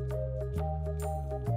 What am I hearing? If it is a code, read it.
There is no code.